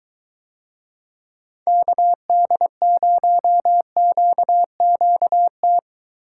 Morse code, heard as a keyed tone, about 22 wpm, 700 Hz. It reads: KD0QQT